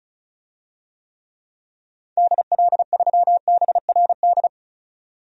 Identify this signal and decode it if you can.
Morse code, heard as a keyed tone, about 36 wpm, 700 Hz. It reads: DL3BRD
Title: DXCC-100 Call Signs - 6 Characters - 35wpm